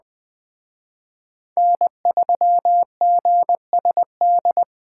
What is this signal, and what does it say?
Morse code, heard as a keyed tone, about 21 wpm, 700 Hz. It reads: N3GSD
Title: Speed Racing - USA 1x3 Call Signs 17wpm